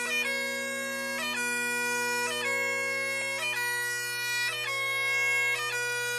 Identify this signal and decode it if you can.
Emergency Alert System (SAME) tones (header bursts), heard as harsh buzzing bursts. It is heard from 0.2 to 5.6 s.
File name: wrct